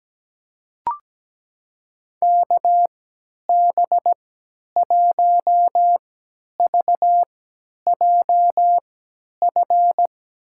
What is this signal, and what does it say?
Morse code, heard as a keyed tone, about 17 wpm, spaced as for 10 wpm, 700 Hz. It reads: KB1VJF
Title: USA Call Signs - 6 Characters - 3x Character Spacing - 17wpm